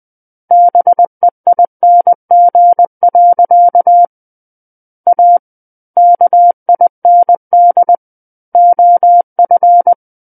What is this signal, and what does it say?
Morse code, heard as a keyed tone, about 20 wpm, 700 Hz. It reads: BEING. A KIND OF
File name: 20 WPM